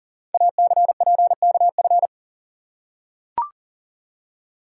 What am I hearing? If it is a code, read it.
Morse code, heard as a keyed tone, about 39 wpm, 700 Hz. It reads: ACPKF
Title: Code Groups - 5 Letter:Numbers - 40wpm